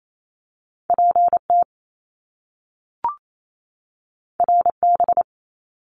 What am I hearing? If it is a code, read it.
Morse code, heard as a keyed tone, about 28 wpm, 700 Hz. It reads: PT RB
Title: Top Two Letter Combinations in Words 28wpm